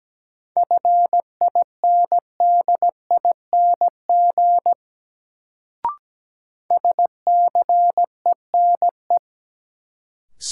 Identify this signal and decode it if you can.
Morse code, heard as a keyed tone, about 17 wpm, 700 Hz. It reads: FINDING SCENE
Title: Sentences from Top 1100 Words - New Word Drill - 17wpm